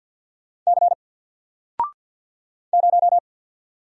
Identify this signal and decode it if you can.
Morse code, heard as a keyed tone, about 48 wpm, 700 Hz. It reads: C 0
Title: Single Letter-Number Alt 50wpm